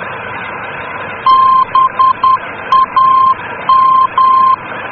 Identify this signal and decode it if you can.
Morse code, heard as a keyed tone, about 10 wpm, 1050 Hz. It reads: BAM
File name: germany / mx / BAM-D